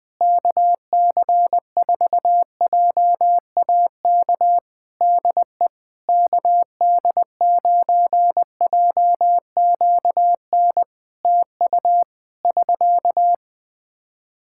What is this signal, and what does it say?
Morse code, heard as a keyed tone, about 20 wpm, 700 Hz. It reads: KC4JAK DE KD9JQN TU <SK>
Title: Practice QSO - Exchange 4 - Ending - 20wpm